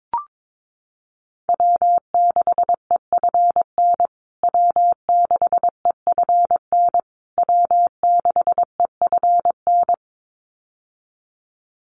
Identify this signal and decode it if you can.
Morse code, heard as a keyed tone, about 22 wpm, 700 Hz. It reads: W6EFN W6EFN W6EFN
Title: USA Call Signs Repeated 3x 22wpm